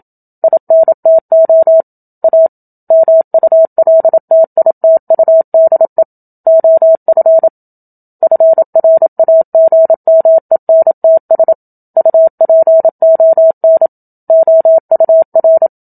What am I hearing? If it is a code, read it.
Morse code, heard as a keyed tone, about 27 wpm, 650 Hz. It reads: INTO A MULTITUDE OF FRAGMENTS UPON OUR
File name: Book Two / 01-Chapter 1 - 27 WPM